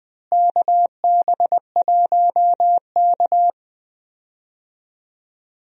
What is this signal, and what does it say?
Morse code, heard as a keyed tone, about 20 wpm, 700 Hz. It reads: KB1K
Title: USA Call Signs - 4 Characters - Repeated 3x - 20wpm